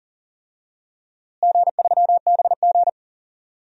Morse code, heard as a keyed tone, about 40 wpm, 700 Hz. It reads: G3BG
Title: DXCC-100 Call Signs - 4 Characters - Repeated 2x - 40wpm